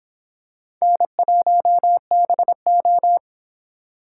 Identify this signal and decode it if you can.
Morse code, heard as a keyed tone, about 25 wpm, 700 Hz. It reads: N1BO